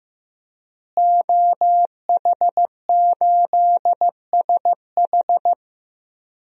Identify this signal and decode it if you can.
Morse code, heard as a keyed tone, about 15 wpm, 700 Hz. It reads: OH8SH
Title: DXCC-100 Call Signs - 5 Characters - Repeated 3x - 15wpm